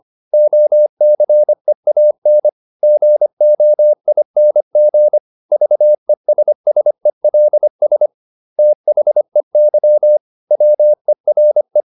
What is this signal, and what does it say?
Morse code, heard as a keyed tone, about 25 wpm, 600 Hz. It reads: OCEAN GOING VESSELS THEY WERE